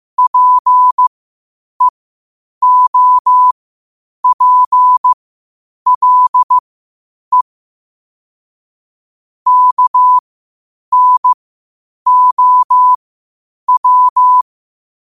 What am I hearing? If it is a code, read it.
Morse code, heard as a keyed tone, about 15 wpm, spaced as for 8 wpm, 1000 Hz. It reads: PEOPLE KNOW
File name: qotdfiles / 8wpm